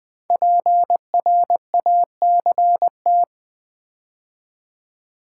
Morse code, heard as a keyed tone, about 20 wpm, 700 Hz. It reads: PRACT
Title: Build A Word - 5 Letters At A Time - 12-Letter Words - 20wpm